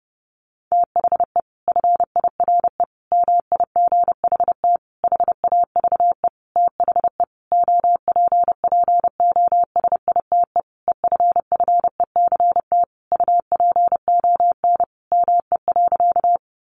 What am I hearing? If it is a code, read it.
Morse code, heard as a keyed tone, about 30 wpm, 700 Hz. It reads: THE FIRE MIGHT HAVE THE OPPOSITE EFFECT UPON ME.
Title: Sentences from Top 1300 Words 30wpm